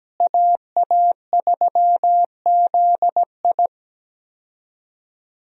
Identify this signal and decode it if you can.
Morse code, heard as a keyed tone, about 17 wpm, 700 Hz. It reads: AA3ZI
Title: USA Call Signs - 5 Characters - Repeated 3x - 17wpm